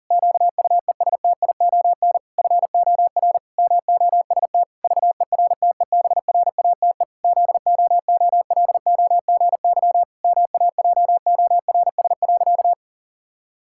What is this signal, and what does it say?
Morse code, heard as a keyed tone, about 37 wpm, 700 Hz. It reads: QUESTION FOR MOST VERTEBRATE ZOOLOGY MAJORS.
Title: Pangram Challenge v2 - 40wpm